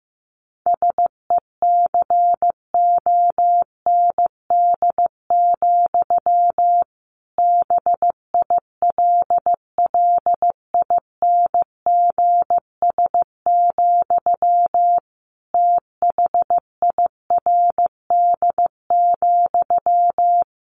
Morse code, heard as a keyed tone, about 15 wpm, 700 Hz. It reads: SECOND, BILLINGS, THIRD,
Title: A Sound Of Thunder 15wpm